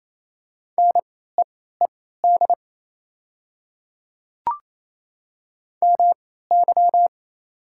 Morse code, heard as a keyed tone, about 28 wpm, spaced as for 16 wpm, 700 Hz. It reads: NEED MY
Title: 3x Character Spacing - Top 300 Words - 28wpm